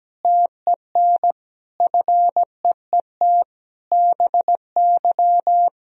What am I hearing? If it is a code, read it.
Morse code, heard as a keyed tone, about 17 wpm, 700 Hz. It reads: TEN FEET BY
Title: Sentences from Top 1000 Words - Review Part 2 - 17wpm